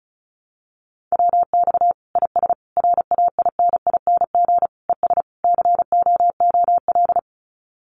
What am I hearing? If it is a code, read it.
Morse code, heard as a keyed tone, about 34 wpm, 700 Hz. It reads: WX IS RAINING ES COOL
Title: Common QSO Phrases 35wpm